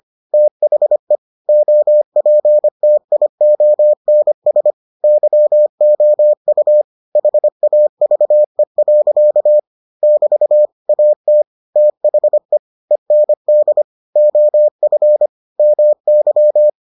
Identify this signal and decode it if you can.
Morse code, heard as a keyed tone, about 25 wpm, 600 Hz. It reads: THE OPTIONS YOU HAVE. <BT> AT THE END OF MY